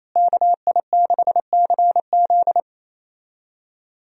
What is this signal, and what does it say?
Morse code, heard as a keyed tone, about 28 wpm, 700 Hz. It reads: KI6CZ
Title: USA Call Signs - 5 Characters - Repeated 2x - 28wpm